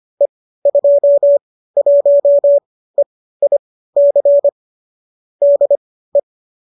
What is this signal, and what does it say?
Morse code, heard as a keyed tone, about 25 wpm, spaced as for 15 wpm, 570 Hz. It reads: E21EIC DE